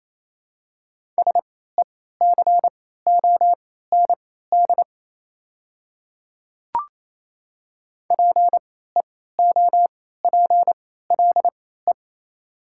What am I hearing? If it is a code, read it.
Morse code, heard as a keyed tone, about 28 wpm, spaced as for 16 wpm, 700 Hz. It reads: SECOND PEOPLE
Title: 3x Character Spacing - Top 300 Words - 28wpm